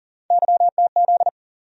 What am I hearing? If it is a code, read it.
Morse code, heard as a keyed tone, about 40 wpm, 700 Hz. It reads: YTZ
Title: USA Call Sign Postfixes - 3 Letters 40wpm